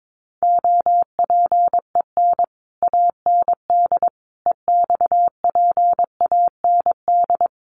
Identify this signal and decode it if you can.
Morse code, heard as a keyed tone, about 22 wpm, 700 Hz. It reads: OPEN AND EXPAND
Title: Sentences from Top 1400 Words 22wpm